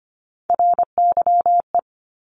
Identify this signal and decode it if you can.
Morse code, heard as a keyed tone, about 25 wpm, 700 Hz. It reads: RYE
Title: Multi-letter Phonemes Repeated 3x 25wpm